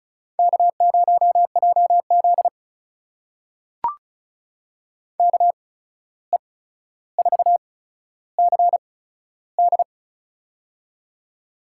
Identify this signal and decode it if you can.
Morse code, heard as a keyed tone, about 36 wpm, spaced as for 14 wpm, 700 Hz. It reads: K0JZ KE4CD
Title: USA Call Signs - 8x Character Spacing - 35wpm